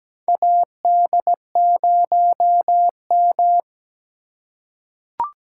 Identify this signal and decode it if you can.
Morse code, heard as a keyed tone, about 17 wpm, 700 Hz. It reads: AD0M
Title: USA Call Signs - 4 Characters - 2x Character Spacing - 17wpm